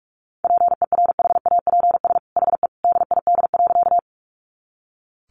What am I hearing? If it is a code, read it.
Morse code, heard as a keyed tone, about 44 wpm, 700 Hz. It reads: PERHAPS HE DID.
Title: Sentences from Top 500 Words 45wpm